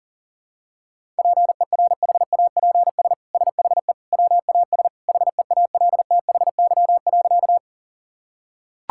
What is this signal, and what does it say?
Morse code, heard as a keyed tone, about 40 wpm, 700 Hz. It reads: PERHAPS SHE WAS HEALTHY.
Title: Sentences from Top 1000 Words - Review Part 5 - 40wpm